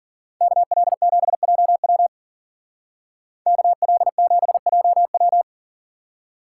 Morse code, heard as a keyed tone, about 49 wpm, 700 Hz. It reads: KL7JW KL7JW
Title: Speed Racing - USA 2x2 Call Signs 40wpm